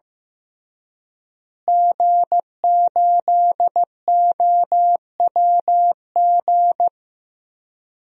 Morse code, heard as a keyed tone, about 15 wpm, 700 Hz. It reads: G8OWG
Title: DXCC-100 Call Signs - 5 Characters - Repeated 3x - 15wpm